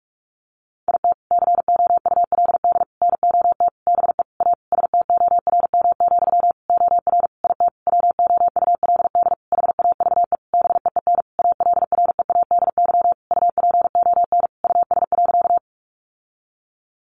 Morse code, heard as a keyed tone, about 44 wpm, 700 Hz. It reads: IT COULD NOT BE A STORM, OR IT WOULD HAVE BEEN ALREADY UPON US.
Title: Sentences from Top 1500 Words 45wpm